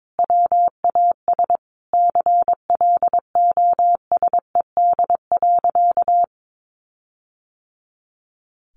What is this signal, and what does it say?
Morse code, heard as a keyed tone, about 22 wpm, 700 Hz. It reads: WAS CLOSED.
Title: Sentences from Top 1400 Words 22wpm